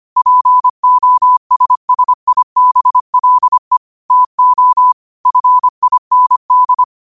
Morse code, heard as a keyed tone, about 25 wpm, 1000 Hz. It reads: POSSIBLE TO FIND
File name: qotdfiles / 25wpm